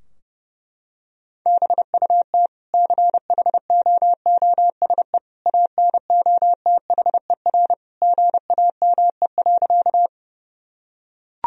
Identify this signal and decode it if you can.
Morse code, heard as a keyed tone, about 29 wpm, 700 Hz. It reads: BUT CHOOSE ANOTHER GAME.
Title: Sentences from Top 1000 Words - Review Part 4 - 30wpm